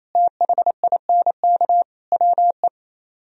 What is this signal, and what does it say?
Morse code, heard as a keyed tone, about 34 wpm, 700 Hz. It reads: THINK WE
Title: Speed-Racing - Sets of 4 Words 28wpm